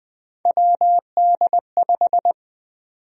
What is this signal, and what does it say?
Morse code, heard as a keyed tone, about 21 wpm, 700 Hz. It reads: WD5
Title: USA Call Sign Prefixes - 20wpm